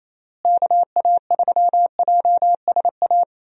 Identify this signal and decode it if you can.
Morse code, heard as a keyed tone, about 28 wpm, 700 Hz. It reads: KA3JSA
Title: USA Call Signs - 6 Characters - Repeated 3x - 28wpm